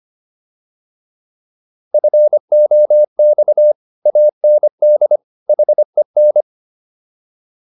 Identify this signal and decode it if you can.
Morse code, heard as a keyed tone, about 25 wpm, 600 Hz. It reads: FOX AND HEN